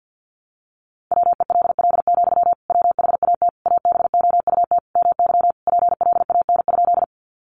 Morse code, heard as a keyed tone, about 50 wpm, 700 Hz. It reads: WELL, WHAT ABOUT MY PLAN?
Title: Sentences from Top 500 Words 50wpm